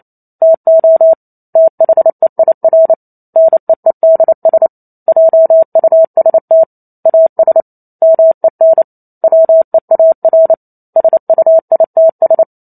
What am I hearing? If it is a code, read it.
Morse code, heard as a keyed tone, about 28 wpm, 650 Hz. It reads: TO THEIR NEEDS JUST AS MEN WEAR SUITS
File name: Chapter 2 - 28 WPM